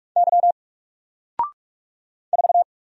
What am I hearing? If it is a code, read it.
Morse code, heard as a keyed tone, about 43 wpm, 700 Hz. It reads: Y 4
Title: Single Character (Letter:Number:Punctuation:Prosign) Alt 45wpm